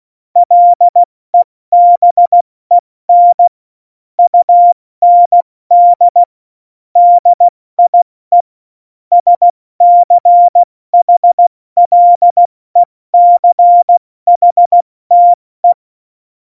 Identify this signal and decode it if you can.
Morse code, heard as a keyed tone, about 16 wpm, 700 Hz. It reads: LEBEN UND DIE SCHLECHTE
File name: Morsekurs / Witze / Witz 22. September 2020, 70 F80